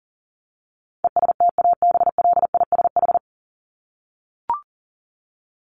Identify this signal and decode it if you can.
Morse code, heard as a keyed tone, about 40 wpm, 700 Hz. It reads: ESTABLISH
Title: ED Suffix - Top 100 Words with Base Words 40wpm